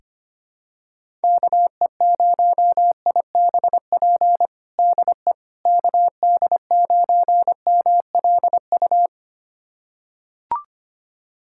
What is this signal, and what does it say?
Morse code, heard as a keyed tone, about 25 wpm, 700 Hz. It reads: KE0IBP DE KD9MLU
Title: USA Callsign DE Callsign - 6 Characters - 25wpm